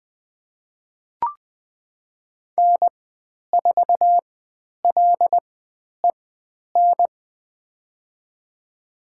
Morse code, heard as a keyed tone, about 20 wpm, spaced as for 10 wpm, 700 Hz. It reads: N4LEN
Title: 095 Lesson 16g - Letter c - Call Signs 20wpm